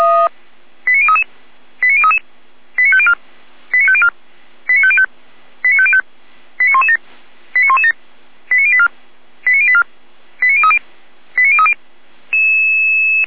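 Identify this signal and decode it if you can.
Five-tone selective call (ZVEI), heard as quick runs of five tones, tones of about 70 ms. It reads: 89033 89033 89504 89504 89505 89505 89117 89117 89084 89084 89030 89030
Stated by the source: ZVEI five-tone call 89080, received by SDR.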